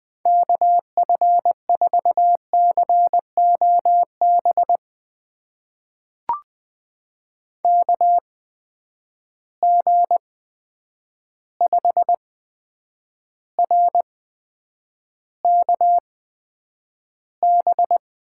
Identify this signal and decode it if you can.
Morse code, heard as a keyed tone, about 20 wpm, spaced as for 9 wpm, 700 Hz. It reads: KF4COB KG5RKB
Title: USA Call Signs - 6 Characters - 8x Character Spacing - 20wpm